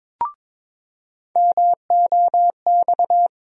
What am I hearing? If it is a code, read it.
Morse code, heard as a keyed tone, about 21 wpm, 700 Hz. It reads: MOX
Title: USA Call Sign Postfixes - 3 Letters 22wpm